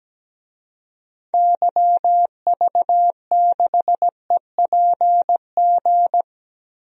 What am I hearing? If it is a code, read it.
Morse code, heard as a keyed tone, about 17 wpm, 700 Hz. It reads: YV6EPG
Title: DXCC-100 Call Signs - 6 Characters - Repeated 2x - 17wpm